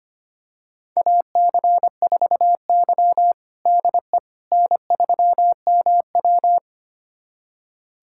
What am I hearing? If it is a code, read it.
Morse code, heard as a keyed tone, about 25 wpm, 700 Hz. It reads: AC4Y DE N3MW
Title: USA Callsign DE Callsign - 4 Characters - 25wpm